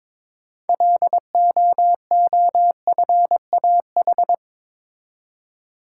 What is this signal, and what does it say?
Morse code, heard as a keyed tone, about 22 wpm, 700 Hz. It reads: LOOFAH
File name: Difficult Words with Hints - 22wpm